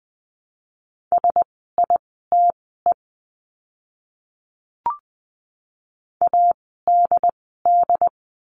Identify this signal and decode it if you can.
Morse code, heard as a keyed tone, about 20 wpm, spaced as for 15 wpm, 700 Hz. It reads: SITE ADD
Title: 101 Lesson 17f - Letter u - Words 20wpm